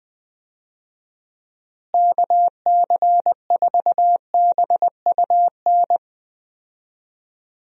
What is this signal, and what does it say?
Morse code, heard as a keyed tone, about 19 wpm, 700 Hz. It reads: KC4BUN
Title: USA Call Signs - 6 Characters - Repeated 2x - 20wpm